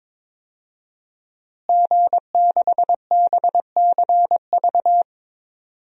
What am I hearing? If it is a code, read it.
Morse code, heard as a keyed tone, about 22 wpm, 700 Hz. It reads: G6BCV
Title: DXCC-100 Call Signs - 5 Characters - Repeated 3x - 22wpm